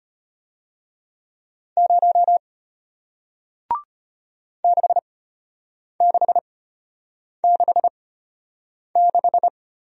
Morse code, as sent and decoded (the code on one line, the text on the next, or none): ----- / -.... -.... -.... -....
0 6666